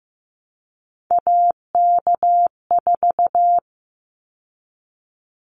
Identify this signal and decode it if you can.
Morse code, heard as a keyed tone, about 15 wpm, 700 Hz. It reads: AK4
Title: Callsign Prefixes for Top 25 DXCC - 15wpm